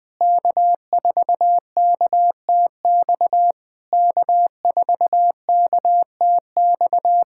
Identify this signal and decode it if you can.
Morse code, heard as a keyed tone, about 22 wpm, 700 Hz. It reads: K4KTX K4KTX
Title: SOTA QSO Practice 20wpm